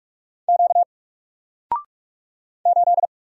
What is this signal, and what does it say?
Morse code, heard as a keyed tone, about 46 wpm, 700 Hz. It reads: Q 8
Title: Single Letter-Number 45wpm